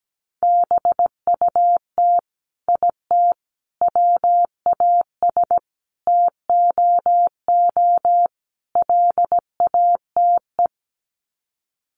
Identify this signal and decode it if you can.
Morse code, heard as a keyed tone, about 17 wpm, 700 Hz. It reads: BUT IT WAS TOO LATE
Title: Sets of 5 Words Encore2 17wpm